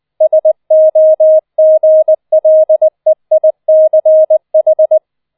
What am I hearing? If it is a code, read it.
Morse code, heard as a keyed tone, about 20 wpm, 630 Hz. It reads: SOGLEICH